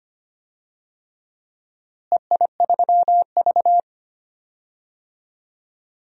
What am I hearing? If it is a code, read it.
Morse code, heard as a keyed tone, about 25 wpm, 700 Hz. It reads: EI3V
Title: DXCC-100 Call Signs - 4 Characters - Repeated 3x - 25wpm